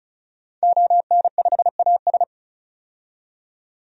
Morse code, heard as a keyed tone, about 36 wpm, 700 Hz. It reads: ON5AS
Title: DXCC-100 Call Signs - 5 Characters - 35wpm